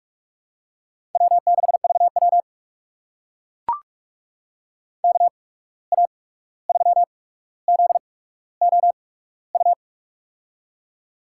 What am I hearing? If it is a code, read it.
Morse code, heard as a keyed tone, about 44 wpm, spaced as for 17 wpm, 690 Hz. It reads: W6VW KA3ZOU
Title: USA Call Signs - 8x Character Spacing - 45wpm